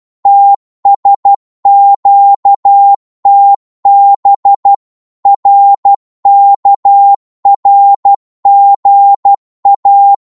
Morse code, heard as a keyed tone, about 12 wpm, 800 Hz. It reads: TSQTB RKRGA
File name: Morse / ltr-12wpm